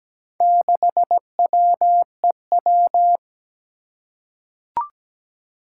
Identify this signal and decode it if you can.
Morse code, heard as a keyed tone, about 17 wpm, 700 Hz. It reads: 6WEW
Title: Four Challenging Letter-Number Combinations 17wpm